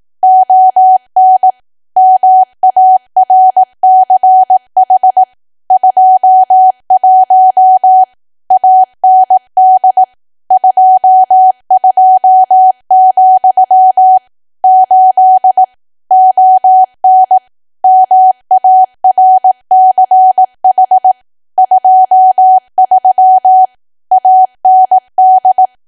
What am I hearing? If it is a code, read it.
Morse code, heard as a keyed tone, about 18 wpm, 750 Hz. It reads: ON MARCH 21 AND 22, 8 ON MARCH 23 AND
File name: cwa / W1AW-Level2 / 18WPM